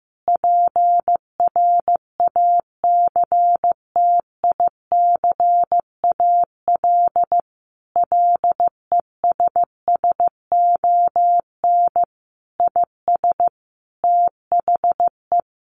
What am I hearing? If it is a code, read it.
Morse code, heard as a keyed tone, about 15 wpm, 700 Hz. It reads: PRACTICAL LESSON IS THE
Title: Sentences from Top 1400 Words 15wpm